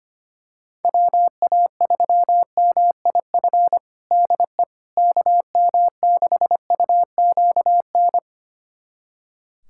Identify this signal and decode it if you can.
Morse code, heard as a keyed tone, about 25 wpm, 700 Hz. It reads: WA3MIF DE KM6UQN
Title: USA Callsign DE Callsign - 6 Characters - 25wpm